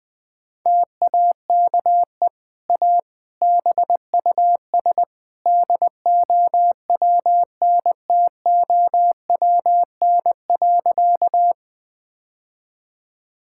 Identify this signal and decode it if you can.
Morse code, heard as a keyed tone, about 20 wpm, 700 Hz. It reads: TAKE A BUS DOWNTOWN.
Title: Easy Sentences with Hints - 20wpm